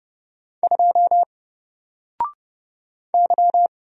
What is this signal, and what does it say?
Morse code, heard as a keyed tone, about 30 wpm, 700 Hz. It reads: 2 Y
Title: Single Character (Letter:Number:Punctuation:Prosign) Alt 30wpm